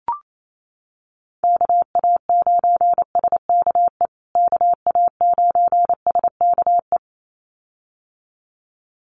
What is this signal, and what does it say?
Morse code, heard as a keyed tone, about 28 wpm, 700 Hz. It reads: KA9SKE KA9SKE
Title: USA Call Signs Repeated 2x 28wpm